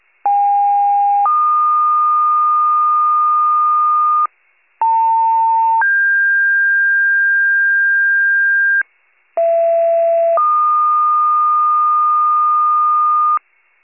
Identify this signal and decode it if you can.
Two-tone page, heard as a one-second tone, then a three-second tone, about 810 then 1220 Hz, about 900 then 1600 Hz, about 670 then 1220 Hz. It is heard from 0.3 to 13.4 s.
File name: wcalert